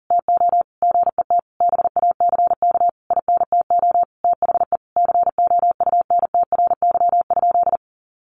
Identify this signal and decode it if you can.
Morse code, heard as a keyed tone, about 40 wpm, 700 Hz. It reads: TO GET BACK INTO THE COUNTRY?
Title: Sentences from Top 1400 Words 40wpm